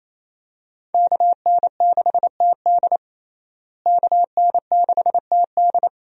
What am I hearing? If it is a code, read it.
Morse code, heard as a keyed tone, about 28 wpm, 700 Hz. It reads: KN6TD KN6TD
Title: USA Call Signs - 5 Characters - Repeated 2x - 28wpm